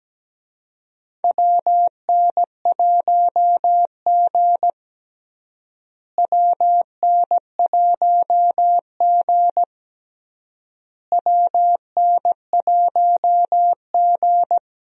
Morse code, heard as a keyed tone, about 17 wpm, 700 Hz. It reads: WN1G WN1G WN1G
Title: USA Call Signs - 4 Characters - Repeated 3x - 17wpm